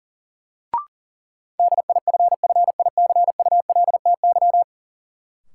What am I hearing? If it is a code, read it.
Morse code, heard as a keyed tone, about 40 wpm, 700 Hz. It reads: DIFFICULTY
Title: Sentences from Top 1500 Words - New Word Drill - 40wpm